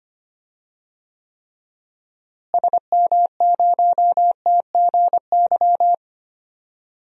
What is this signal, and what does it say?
Morse code, heard as a keyed tone, about 25 wpm, 700 Hz. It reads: SM0TGY
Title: DXCC-100 Call Signs - 6 Characters - Repeated 2x - 25wpm